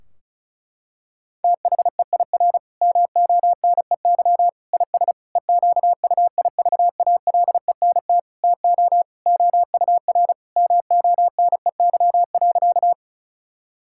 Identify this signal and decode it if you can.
Morse code, heard as a keyed tone, about 36 wpm, 690 Hz. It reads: THEIR MONEY IS EQUIVALENT TO OUR MONEY.
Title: Sentences from Top 1600 Words 35wpm